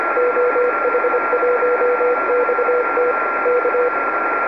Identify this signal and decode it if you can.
Morse code, heard as a keyed tone, about 26 wpm, 490 Hz, with radio noise. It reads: OH1XT K
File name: oh1xt-2m